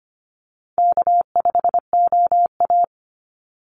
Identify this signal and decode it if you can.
Morse code, heard as a keyed tone, about 25 wpm, 700 Hz. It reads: K5OA
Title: Easy Call Signs Repeated 2x - 25wpm